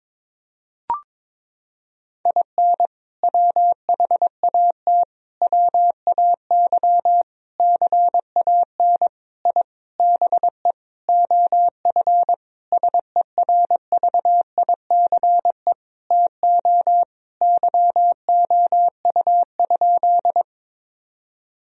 Morse code, heard as a keyed tone, about 21 wpm, 700 Hz. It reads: IN WHAT WAY CAN I BE OF SERVICE TO YOU?
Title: Sentences from Top 1000 Words - Review Part 4 - 22wpm